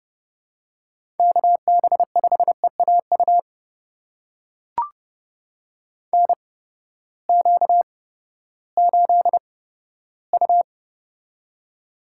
Morse code, heard as a keyed tone, about 30 wpm, spaced as for 15 wpm, 700 Hz. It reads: KB5EAU NQ8U
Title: USA Call Signs - 8x Character Spacing - 30wpm